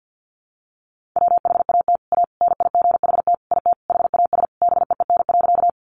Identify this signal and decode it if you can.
Morse code, heard as a keyed tone, about 50 wpm, 700 Hz. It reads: WHAT A NIGHT IT HAS BEEN.
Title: Sentences from Top 300 Words 50wpm